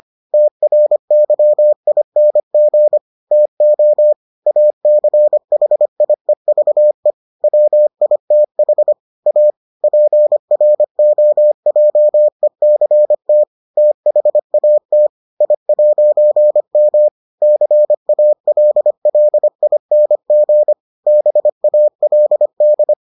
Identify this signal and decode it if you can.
Morse code, heard as a keyed tone, about 25 wpm, 600 Hz. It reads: TRYING TO ACHIEVE WITH A PROJECT THAT I'M CALLING BALD